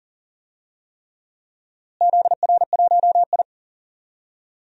Morse code, heard as a keyed tone, about 40 wpm, 700 Hz. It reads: ZR1I
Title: DXCC-100 Call Signs - 4 Characters - Repeated 3x - 40wpm